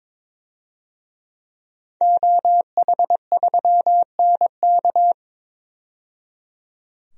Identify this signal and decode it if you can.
Morse code, heard as a keyed tone, about 22 wpm, 700 Hz. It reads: OH3NK
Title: DXCC-100 Call Signs - 5 Characters - 22wpm